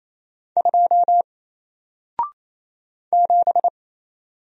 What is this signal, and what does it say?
Morse code, heard as a keyed tone, about 28 wpm, 700 Hz. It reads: 2 7
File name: Single-Letter-Number-28wpm